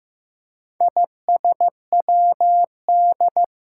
Morse code, heard as a keyed tone, about 15 wpm, 700 Hz. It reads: ISWD
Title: Four Challenging Letter-Number Combinations 15wpm